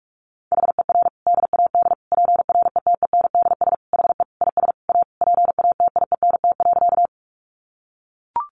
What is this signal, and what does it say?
Morse code, heard as a keyed tone, about 44 wpm, 700 Hz. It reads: HER DAD PRETENDS HE IS A PATIENT.
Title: Additional Easy Phrases 45wpm